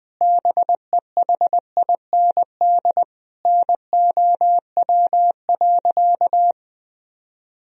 Morse code, heard as a keyed tone, about 21 wpm, 700 Hz. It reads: BEHIND NOW.
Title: Sentences from Top 1000 Words - Review Part 1 - 20wpm